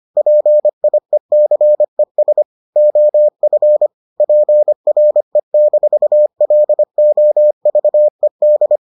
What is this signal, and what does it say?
Morse code, heard as a keyed tone, about 25 wpm, 600 Hz. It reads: PIECES OF PRE-LOVED